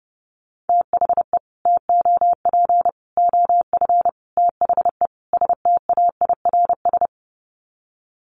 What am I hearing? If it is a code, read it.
Morse code, heard as a keyed tone, about 28 wpm, 700 Hz. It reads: THE TOP OF THE STAIRS
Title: Sets of 5 Words Encore 30wpm